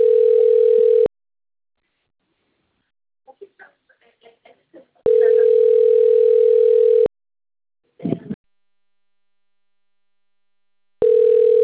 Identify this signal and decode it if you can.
Ringback tone (North America), 440 + 480 Hz.